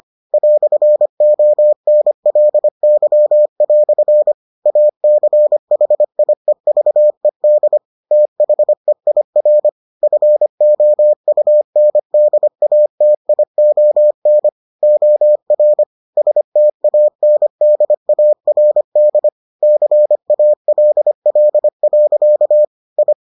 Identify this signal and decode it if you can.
Morse code, heard as a keyed tone, about 25 wpm, 600 Hz. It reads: "ONLY" ACHIEVED THEIR FOUNDATION OR STANDARD CALL. I